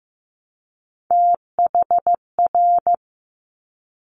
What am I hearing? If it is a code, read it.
Morse code, heard as a keyed tone, about 15 wpm, 700 Hz. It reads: THR
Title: Top Three Letter Combinations in Words 15wpm